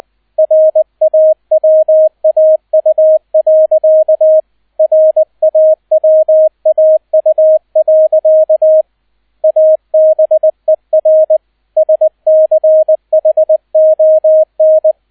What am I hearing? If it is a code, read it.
Morse code, heard as a keyed tone, about 20 wpm, 630 Hz, with radio noise. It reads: RAWAU. RAWAU. ABER SCHON
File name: 002 Zweiter Streich 20wpm